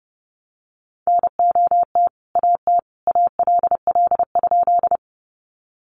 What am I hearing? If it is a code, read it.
Morse code, heard as a keyed tone, about 30 wpm, 700 Hz. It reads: NOT AT ALL?
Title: Sentences from Top 600 Words 30wpm